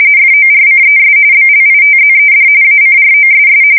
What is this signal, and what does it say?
RTTY, about 45.45 baud. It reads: FORCES BLITZED ACROSS
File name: fsk170 / battleshipnj